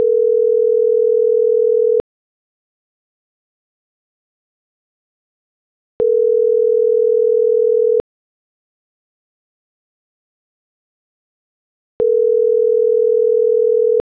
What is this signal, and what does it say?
Ringback tone (North America), 440 + 480 Hz.